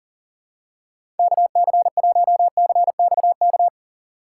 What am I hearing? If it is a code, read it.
Morse code, heard as a keyed tone, about 40 wpm, 700 Hz. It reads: KC1CXK
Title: USA Call Signs - 6 Characters - 3x Character Spacing - 40wpm